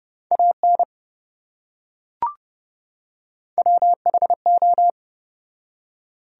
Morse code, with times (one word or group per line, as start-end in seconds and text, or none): 0.3-0.8 AN
3.6-4.9 WHO